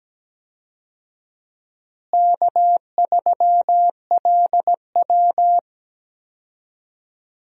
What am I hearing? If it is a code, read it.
Morse code, heard as a keyed tone, about 17 wpm, 700 Hz. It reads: K3LW
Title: USA Call Signs - 4 Characters - Repeated 3x - 17wpm